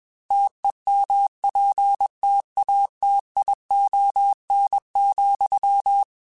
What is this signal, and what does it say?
Morse code, heard as a keyed tone, about 21 wpm, 800 Hz. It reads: TEMPTATION,